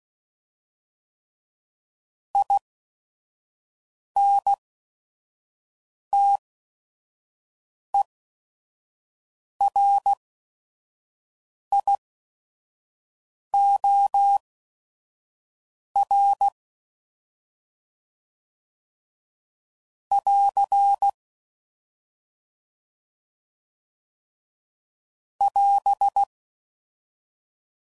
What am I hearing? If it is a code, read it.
Morse code, heard as a keyed tone, about 16 wpm, spaced as for 5 wpm, 800 Hz. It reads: INTERIOR <AR> <AS>
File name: Prefixes / By First Letter / Is